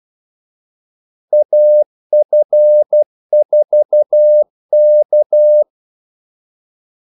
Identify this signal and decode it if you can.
Morse code, heard as a keyed tone, about 12 wpm, 600 Hz. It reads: AF4K
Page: 2 ekstra ordmellemrum mellem kaldesignalerne